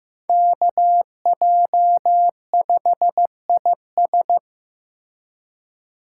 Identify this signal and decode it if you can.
Morse code, heard as a keyed tone, about 18 wpm, 700 Hz. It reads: KJ5IS
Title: Speed Racing - USA 2x2 Call Signs 15wpm